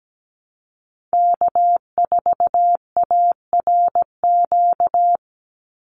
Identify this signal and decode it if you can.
Morse code, heard as a keyed tone, about 17 wpm, 700 Hz. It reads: K4ARQ
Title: Call Signs Repeated 3x - 17wpm